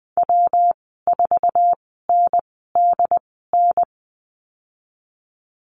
Morse code, heard as a keyed tone, about 20 wpm, spaced as for 14 wpm, 700 Hz. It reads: W4NDN
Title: 122 Lesson 20g - Number 3 - Call Signs 20wpm